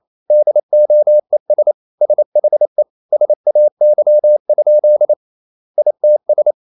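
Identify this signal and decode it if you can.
Morse code, heard as a keyed tone, about 28 wpm, 600 Hz. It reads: DOES SHE SAY? ITS
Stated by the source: Morse code practice at 28 wpm, text of Ken Follett's novel Never in English, from CW med Gnister.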